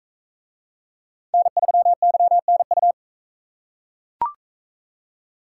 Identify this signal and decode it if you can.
Morse code, heard as a keyed tone, about 43 wpm, 700 Hz. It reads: N3YNU